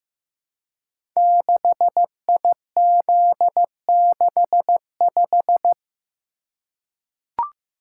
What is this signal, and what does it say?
Morse code, heard as a keyed tone, about 15 wpm, 700 Hz. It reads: 6IZ65
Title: Code Groups - 5 Letter:Numbers - 15wpm